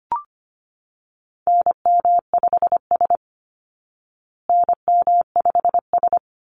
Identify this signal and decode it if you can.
Morse code, heard as a keyed tone, about 25 wpm, 700 Hz. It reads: NM5S NM5S
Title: Call Signs Repeated 3x - 25wpm